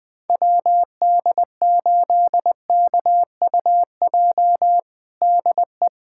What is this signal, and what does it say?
Morse code, heard as a keyed tone, about 21 wpm, 700 Hz. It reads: WD8KUJ DE
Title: USA Callsign DE Callsign 20wpm